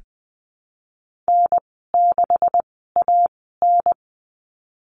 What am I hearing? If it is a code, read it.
Morse code, heard as a keyed tone, about 20 wpm, spaced as for 15 wpm, 700 Hz. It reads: N6AN
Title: 179 Lesson 29g - Slash - Call Signs 20wpm